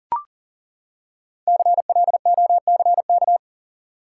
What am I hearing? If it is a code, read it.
Morse code, heard as a keyed tone, about 37 wpm, 700 Hz. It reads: CLOCK
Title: Five Letter Words - 40wpm